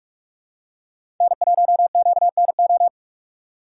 Morse code, heard as a keyed tone, about 46 wpm, 700 Hz. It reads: N1QNO
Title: USA Call Signs - 5 Characters - 3x Character Spacing - 45wpm